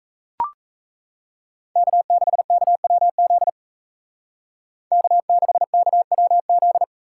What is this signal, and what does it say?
Morse code, heard as a keyed tone, about 34 wpm, 700 Hz. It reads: K6KWZ K6KWZ
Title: Speed Racing - USA 1x3 Call Signs 28wpm